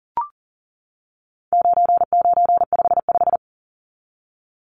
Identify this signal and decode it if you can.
Morse code, heard as a keyed tone, about 40 wpm, 700 Hz. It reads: 9955
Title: Numbers - 4-Digit 40wpm